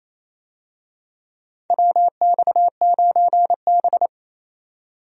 Morse code, heard as a keyed tone, about 27 wpm, 700 Hz. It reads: WX9B
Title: USA Call Signs - 4 Characters - 28wpm